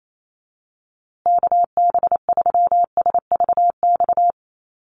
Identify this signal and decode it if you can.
Morse code, heard as a keyed tone, about 28 wpm, 700 Hz. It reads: KB3SVX
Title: USA Call Signs Repeated 2x 28wpm